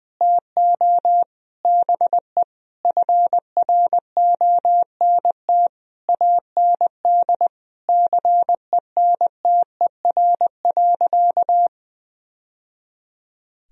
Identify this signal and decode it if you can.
Morse code, heard as a keyed tone, about 20 wpm, 700 Hz. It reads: TO BE FRONT AND CENTER.
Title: Sentences from Top 1000 Words - Review Part 2 - 20wpm